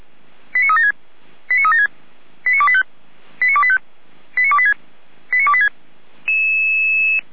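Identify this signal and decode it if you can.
Five-tone selective call (ZVEI), heard as quick runs of five tones, tones of about 70 ms. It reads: 89376 89376 89285 89285 89286 89286
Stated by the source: ZVEI five-tone selective call received by SDR.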